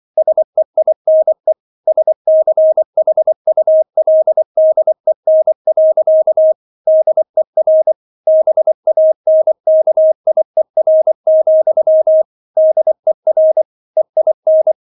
Morse code, heard as a keyed tone, about 24 wpm, 630 Hz. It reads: SEINE SCHULDEN. DER BANKIER, DER EIN